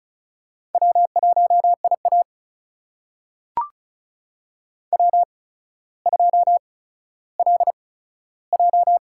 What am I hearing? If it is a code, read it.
Morse code, heard as a keyed tone, about 36 wpm, spaced as for 15 wpm, 700 Hz. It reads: W1IA W2LJ
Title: USA Call Signs - 4 Characters - 8x Character Spacing - 35wpm